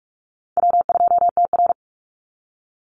Morse code, heard as a keyed tone, about 44 wpm, 700 Hz. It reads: W2TR
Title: Easy Call Signs Repeated 2x - 45wpm